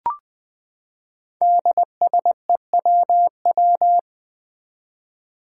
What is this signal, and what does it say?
Morse code, heard as a keyed tone, about 20 wpm, 700 Hz. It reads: DSEWW
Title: Five Challenging Letter-Number Combinations 20wpm